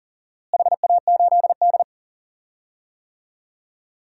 Morse code, heard as a keyed tone, about 37 wpm, 700 Hz. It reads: HA8D
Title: DXCC-100 Call Signs - 4 Characters - Repeated 3x - 40wpm